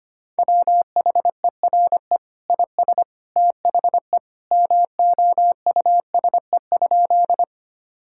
Morse code, heard as a keyed tone, about 25 wpm, 700 Hz. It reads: WHERE IS THE MOUSE?
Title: Sentences from Top 1000 Words - Review Part 5 - 25wpm